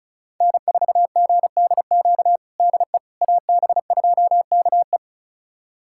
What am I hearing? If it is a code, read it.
Morse code, heard as a keyed tone, about 36 wpm, 700 Hz. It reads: N4GDQ DE AB2KE
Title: USA Callsign DE Callsign - 5 Characters - 35wpm